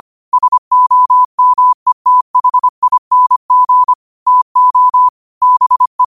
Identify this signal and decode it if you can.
Morse code, heard as a keyed tone, about 25 wpm, 1000 Hz. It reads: SOMETHING TO BE